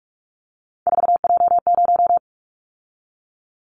Morse code, heard as a keyed tone, about 44 wpm, 700 Hz. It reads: 4J0
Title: Callsign Prefixes for Top 100 DXCC - 45wpm